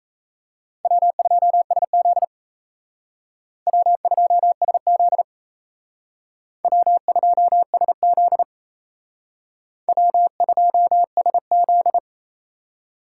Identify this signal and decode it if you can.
Morse code, heard as a keyed tone, about 36 wpm, 700 Hz. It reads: W2SZ W2SZ W2SZ W2SZ